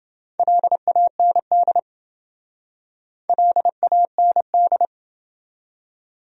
Morse code, heard as a keyed tone, about 28 wpm, 700 Hz. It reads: LAND LAND